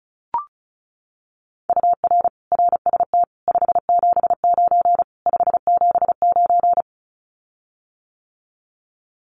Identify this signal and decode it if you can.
Morse code, heard as a keyed tone, about 34 wpm, 700 Hz. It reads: UR RST 579 579
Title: Common QSO Phrases Encore 35wpm